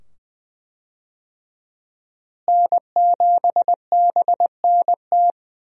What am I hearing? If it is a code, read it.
Morse code, heard as a keyed tone, about 20 wpm, 700 Hz. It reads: N7BNT
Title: USA Call Signs - 5 Characters - Repeated 2x - 20wpm